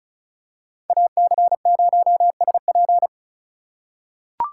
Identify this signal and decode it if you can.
Morse code, heard as a keyed tone, about 36 wpm, 700 Hz. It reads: AC0SP
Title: USA Call Signs - 2x Character Spacing - 35wpm